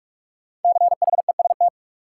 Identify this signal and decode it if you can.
Morse code, heard as a keyed tone, about 44 wpm, 700 Hz. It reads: CHEST
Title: Five Letter Words - 45wpm